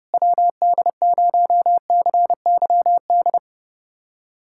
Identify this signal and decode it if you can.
Morse code, heard as a keyed tone, about 29 wpm, 700 Hz. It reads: WD0CYD
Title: USA Callsign DE Callsign - 6 Characters - 30wpm